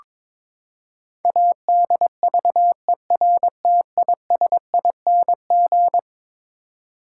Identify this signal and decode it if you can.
Morse code, heard as a keyed tone, about 22 wpm, 700 Hz. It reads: ADVERTISING
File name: practice / 22wpm / Eleven Letter Words - 22wpm